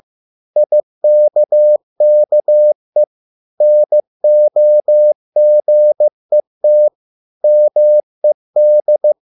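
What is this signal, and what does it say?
Morse code, heard as a keyed tone, about 15 wpm, 600 Hz. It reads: IKKE NOGET MED
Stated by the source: Morse code at a slow 15 wpm from CW med Gnister.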